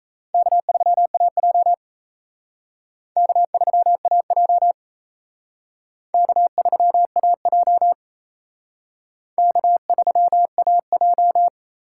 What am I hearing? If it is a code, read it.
Morse code, heard as a keyed tone, about 35 wpm, 700 Hz. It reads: K3AJ K3AJ K3AJ K3AJ
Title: Speed Racing - USA 1x2 Call Signs 28wpm